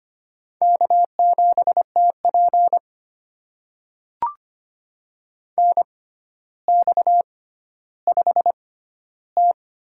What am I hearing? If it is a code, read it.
Morse code, heard as a keyed tone, about 25 wpm, spaced as for 13 wpm, 700 Hz. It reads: K7TP NX5T
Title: USA Call Signs - 4 Characters - 6x Character Spacing - 25wpm